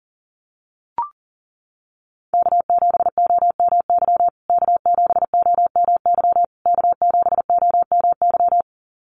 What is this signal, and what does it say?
Morse code, heard as a keyed tone, about 40 wpm, 700 Hz. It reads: K7OMY K7OMY K7OMY
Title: USA Call Signs Repeated 3x 40wpm